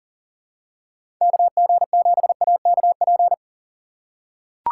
Morse code, heard as a keyed tone, about 40 wpm, 700 Hz. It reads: KG7AKP
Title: USA Call Signs - 6 Characters - 6x Character Spacing - 40wpm